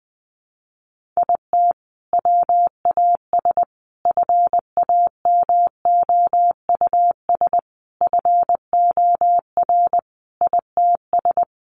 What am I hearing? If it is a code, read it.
Morse code, heard as a keyed tone, about 20 wpm, 700 Hz. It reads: IT WAS FAMOUS FOR ITS
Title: Sentences from Top 1500 Words 20wpm